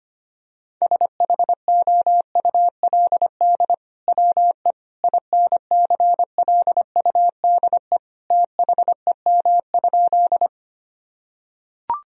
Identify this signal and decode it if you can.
Morse code, heard as a keyed tone, about 25 wpm, 700 Hz. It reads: SHOULD WE INCLUDE THEM?
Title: Sentences from Top 1000 Words - Review Part 3 - 25wpm